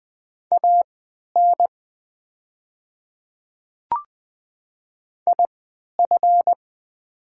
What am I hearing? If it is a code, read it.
Morse code, heard as a keyed tone, about 20 wpm, spaced as for 11 wpm, 700 Hz. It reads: AN IF
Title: 3x Character Spacing - Top 100 Words - 20wpm